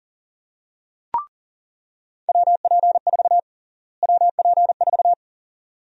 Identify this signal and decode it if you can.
Morse code, heard as a keyed tone, about 40 wpm, 700 Hz. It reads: WP4 WP4
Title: USA Call Sign Prefixes - Repeated 2x - 40wpm